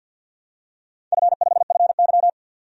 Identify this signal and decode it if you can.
Morse code, heard as a keyed tone, about 47 wpm, 700 Hz. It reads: F5FY
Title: DXCC-100 Call Signs - 4 Characters - 50wpm